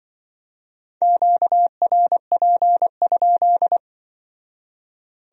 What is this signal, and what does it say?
Morse code, heard as a keyed tone, about 25 wpm, 700 Hz. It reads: QRP?
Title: Speed-Racing Q-Codes - 20wpm